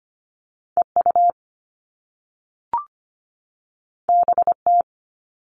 Morse code, heard as a keyed tone, about 25 wpm, 700 Hz. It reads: EU BT
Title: All Cut Numbers - 2-Digit 25wpm